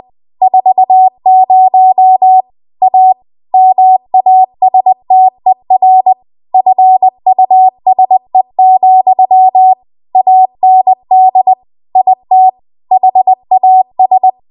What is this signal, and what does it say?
Morse code, heard as a keyed tone, about 20 wpm, 750 Hz. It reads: 40 A MASTER FUSE, AND IT HAS
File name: Morse / 20WPM